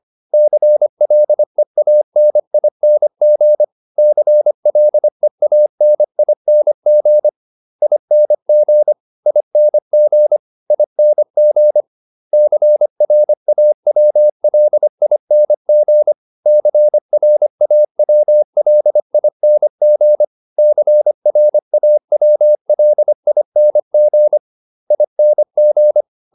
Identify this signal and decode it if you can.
Morse code, heard as a keyed tone, about 25 wpm, 600 Hz. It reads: CLEANING CLEANING ING ING ING CRAWLING CRAWLING CRAWLING ING
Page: advanced-practice-files